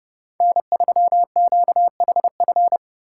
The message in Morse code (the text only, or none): N3QHF